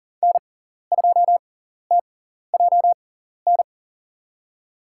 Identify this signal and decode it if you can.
Morse code, heard as a keyed tone, about 40 wpm, spaced as for 14 wpm, 700 Hz. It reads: N2TJN
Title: USA Call Signs - 5 Characters - 6x Character Spacing - 40wpm